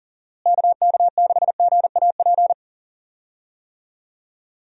Morse code, heard as a keyed tone, about 37 wpm, 700 Hz. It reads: KK6GAP
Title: USA Call Signs - 6 Characters - Repeated 3x - 40wpm